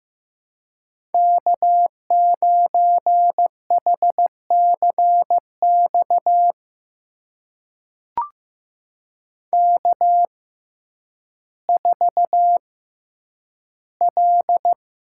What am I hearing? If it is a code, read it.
Morse code, heard as a keyed tone, about 15 wpm, spaced as for 9 wpm, 700 Hz. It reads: K9HCX K4L
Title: USA Call Signs - 5 Characters - 6x Character Spacing - 15wpm